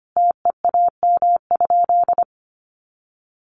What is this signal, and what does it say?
Morse code, heard as a keyed tone, about 25 wpm, 700 Hz. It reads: TEAM?
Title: Sentences from Top 1500 Words 25wpm